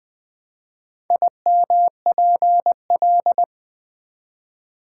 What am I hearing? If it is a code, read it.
Morse code, heard as a keyed tone, about 20 wpm, 700 Hz. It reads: IMPL
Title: Build A Word - 1 Letter At A Time - 3 to 5-Letter Words - 20wpm